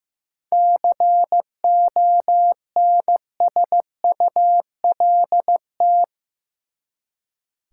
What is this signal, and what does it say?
Morse code, heard as a keyed tone, about 15 wpm, 700 Hz. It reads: CONSULT
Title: Seven Letter Words - 15wpm